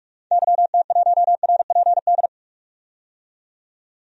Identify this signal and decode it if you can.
Morse code, heard as a keyed tone, about 46 wpm, 700 Hz. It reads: YT1RPD
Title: DXCC-100 Call Signs - 6 Characters - 45wpm